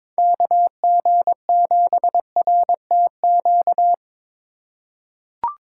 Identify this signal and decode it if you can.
Morse code, heard as a keyed tone, about 22 wpm, 700 Hz. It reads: KG7RTQ
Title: USA Call Signs - 6 Characters - 4x Character Spacing - 22wpm